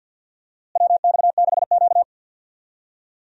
Morse code, heard as a keyed tone, about 49 wpm, 700 Hz. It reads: WX6Q
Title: USA Call Signs - 4 Characters - Repeated 2x - 50wpm